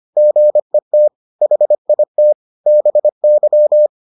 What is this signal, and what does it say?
Morse code, heard as a keyed tone, about 25 wpm, 600 Hz. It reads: GET HIT BY